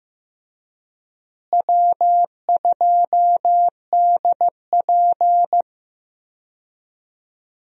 Morse code, heard as a keyed tone, about 15 wpm, 700 Hz. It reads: W2DP
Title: USA Call Signs - 4 Characters - Repeated 3x - 15wpm